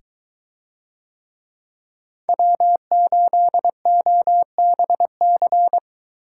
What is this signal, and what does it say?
Morse code, heard as a keyed tone, about 23 wpm, 700 Hz. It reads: W8OBC